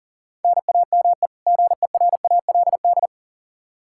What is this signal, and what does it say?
Morse code, heard as a keyed tone, about 40 wpm, 700 Hz. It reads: NAME GERALD
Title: Practice QSO - Exchange 1 - Beginning - 40wpm